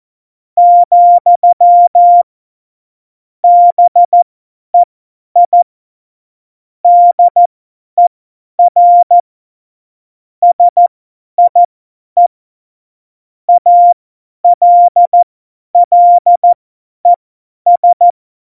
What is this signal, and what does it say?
Morse code, heard as a keyed tone, about 14 wpm, spaced as for 10 wpm, 700 Hz. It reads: , BEI DER SIE ALLES
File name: Morsekurs / Witze / Witz 13 March 2018, 50 F70